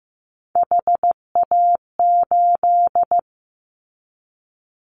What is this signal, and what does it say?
Morse code, heard as a keyed tone, about 15 wpm, 700 Hz. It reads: HA8
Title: Callsign Prefixes for Top 50 DXCC - 15wpm